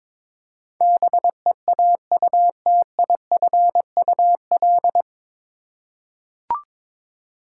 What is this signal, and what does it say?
Morse code, heard as a keyed tone, about 22 wpm, 700 Hz. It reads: BEAUTIFUL
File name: 9 Letter Words - First 3 Letters Given - 22wpm